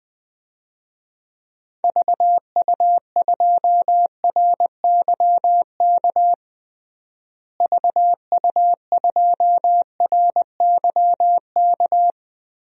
Morse code, heard as a keyed tone, about 21 wpm, 700 Hz. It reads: VU2RYK VU2RYK
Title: DXCC-100 Call Signs - 6 Characters - Repeated 3x - 20wpm